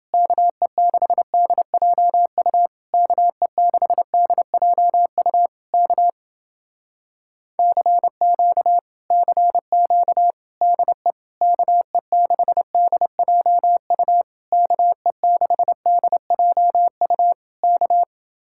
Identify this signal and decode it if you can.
Morse code, heard as a keyed tone, about 30 wpm, 700 Hz. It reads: KE6DJU KE6DJU K CQ CQ DE KE6DJU KE6DJU K